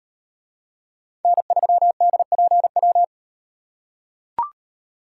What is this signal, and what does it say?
Morse code, heard as a keyed tone, about 37 wpm, 700 Hz. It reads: N3DPW